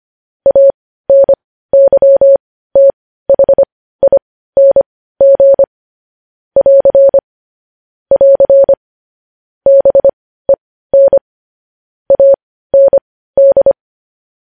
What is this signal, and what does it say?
Morse code, heard as a keyed tone, about 25 wpm, spaced as for 15 wpm, 550 Hz. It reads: ANYTHING <AR> <AR> BEN AND
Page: news-headlines-in-morse-code-at-15-wpm